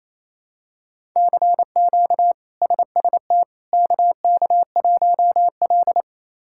Morse code, heard as a keyed tone, about 28 wpm, 700 Hz. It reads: CQ SST KK1L
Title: K1USN SST QSO Practice - 28wpm